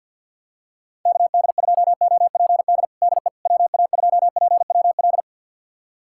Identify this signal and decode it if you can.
Morse code, heard as a keyed tone, about 53 wpm, 700 Hz. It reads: KD2OPD DE WA2PWL
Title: USA Callsign DE Callsign 50wpm